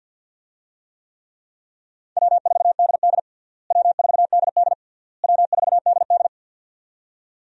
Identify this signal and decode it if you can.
Morse code, heard as a keyed tone, about 46 wpm, 700 Hz. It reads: W4DD W4DD W4DD
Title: USA Call Signs - 4 Characters - Repeated 3x - 50wpm